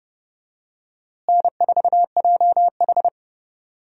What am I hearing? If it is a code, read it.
Morse code, heard as a keyed tone, about 32 wpm, 700 Hz. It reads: N4JH
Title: USA Call Signs - 4 Characters - Repeated 3x - 30wpm